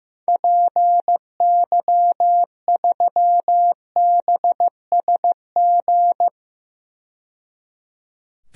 Morse code, heard as a keyed tone, about 15 wpm, 700 Hz. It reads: PY3BSG
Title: DXCC-100 Call Signs - 6 Characters - 15wpm